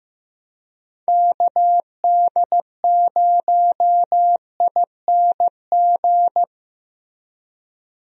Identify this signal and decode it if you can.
Morse code, heard as a keyed tone, about 15 wpm, 700 Hz. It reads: KD0ING
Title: USA Call Signs - 6 Characters - Repeated 2x - 15wpm